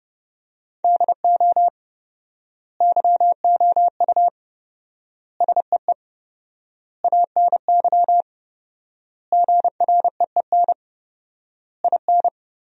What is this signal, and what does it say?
Morse code, heard as a keyed tone, about 30 wpm, 700 Hz. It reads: DO YOU SEE ANY GREEN IN
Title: 4x Word-Spacing - Sentences from Top 500 Words - 30wpm